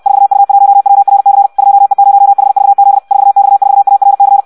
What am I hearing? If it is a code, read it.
Morse code, heard as a keyed tone, about 64 wpm, 800 Hz. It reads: 4U1WRC 2E0HVZ 4X4ARC
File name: Audio practica CW / qrq-tx-66 2